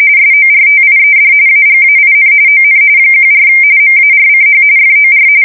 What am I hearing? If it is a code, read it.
RTTY, about 45.45 baud. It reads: SORTIED AGAIN ON 14 MARCH FOR